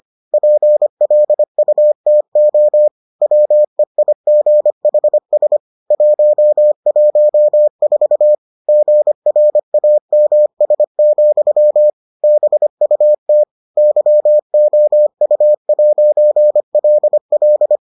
Morse code, heard as a keyed tone, about 25 wpm, 600 Hz. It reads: PLUTO WEIGHS 114 GRAMS, BUT YOU'LL